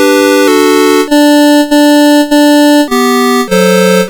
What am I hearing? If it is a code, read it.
There is no code.